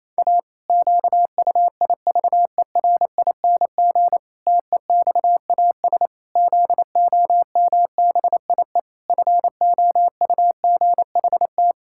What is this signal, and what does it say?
Morse code, heard as a keyed tone, about 28 wpm, 700 Hz. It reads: A QUIVERING TEXAS ZOMBIE FOUGHT
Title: Pangram Challenge v2 - 28wpm